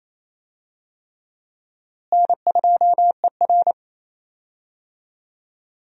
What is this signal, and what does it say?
Morse code, heard as a keyed tone, about 27 wpm, 700 Hz. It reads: N2ER
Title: USA Call Signs - 4 Characters - Repeated 3x - 28wpm